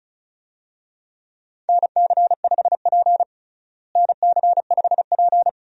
Morse code, heard as a keyed tone, about 34 wpm, 700 Hz. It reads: NC5P NC5P
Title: USA Call Signs - 4 Characters - Repeated 2x - 35wpm